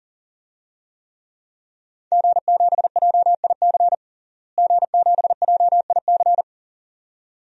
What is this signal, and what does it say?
Morse code, heard as a keyed tone, about 37 wpm, 700 Hz. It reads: G7JIC G7JIC
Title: DXCC-100 Call Signs - 5 Characters - Repeated 2x - 40wpm